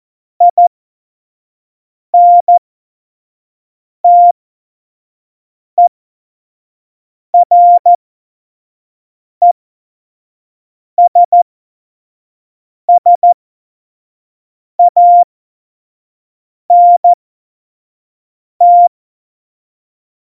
Morse code, heard as a keyed tone, about 14 wpm, spaced as for 5 wpm, 700 Hz. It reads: INTERESSANT
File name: Morsekurs / Witze / Witz 18 August 2020, 25 F70